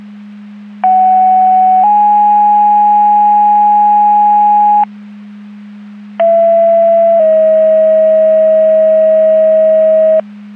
Two-tone page, heard as a one-second tone, then a three-second tone, about 750 then 830 Hz, about 670 then 640 Hz. It is heard from 0.8 to 10.2 s.
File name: Portage County 154.3850 Fire EMS Paging